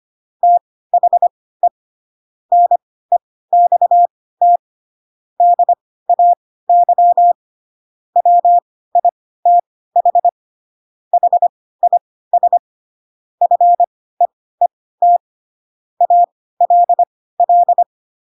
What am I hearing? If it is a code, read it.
Morse code, heard as a keyed tone, about 25 wpm, spaced as for 16 wpm, 700 Hz. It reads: THE NEXT DAY WITH HIS FEET ALL